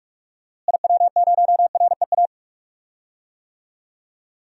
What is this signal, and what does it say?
Morse code, heard as a keyed tone, about 44 wpm, 700 Hz. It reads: IW0REA
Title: DXCC-100 Call Signs - 6 Characters - Repeated 2x - 45wpm